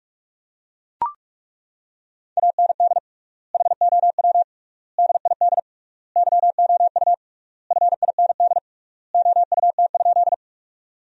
Morse code, heard as a keyed tone, about 41 wpm, 700 Hz. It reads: AND HOW DID YOU FIND OUT?
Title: 3x Word-Spacing - Sentences from Top 200 Words - 45wpm